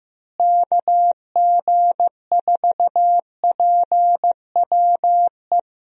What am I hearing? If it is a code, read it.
Morse code, heard as a keyed tone, about 15 wpm, 700 Hz. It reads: KG4PWE
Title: USA Call Signs - 6 Characters - Repeated 3x - 15wpm